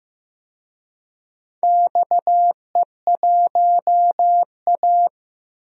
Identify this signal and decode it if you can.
Morse code, heard as a keyed tone, about 15 wpm, 700 Hz. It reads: XE1A
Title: DXCC-100 Call Signs - 4 Characters - 15wpm